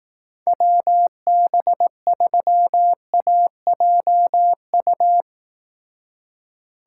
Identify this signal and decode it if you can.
Morse code, heard as a keyed tone, about 18 wpm, 700 Hz. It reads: WB3AJU
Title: Speed Racing - USA 2x3 Call Signs 15wpm